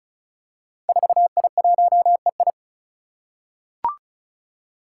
Morse code, heard as a keyed tone, about 34 wpm, 700 Hz. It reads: 4I1EI
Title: Five Challenging Letter-Number Combinations 35wpm